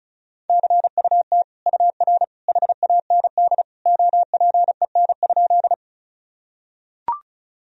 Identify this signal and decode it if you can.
Morse code, heard as a keyed tone, about 36 wpm, 700 Hz. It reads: CUT UR HAND OPEN?
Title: Sentences from Top 300 Words and Abbreviations 35wpm